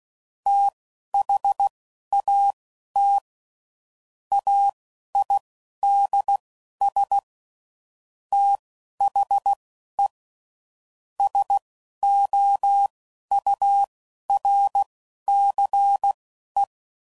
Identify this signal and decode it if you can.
Morse code, heard as a keyed tone, about 16 wpm, spaced as for 11 wpm, 800 Hz. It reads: THAT AIDS THE SOURCE
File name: CW Code Groups @10WPM / Technical Subjects / XL